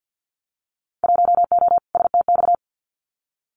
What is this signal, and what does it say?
Morse code, heard as a keyed tone, about 50 wpm, 700 Hz. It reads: 1O STX
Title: Winter Field Day Exchanges 50wpm